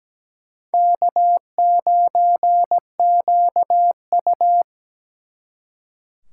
Morse code, heard as a keyed tone, about 20 wpm, 700 Hz. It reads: K9QU
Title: Speed Racing - USA 1x2 Call Signs 17wpm